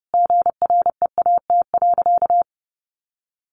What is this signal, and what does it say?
Morse code, heard as a keyed tone, about 30 wpm, 700 Hz. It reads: GREAT.
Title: Sentences from Top 1100 Words 30wpm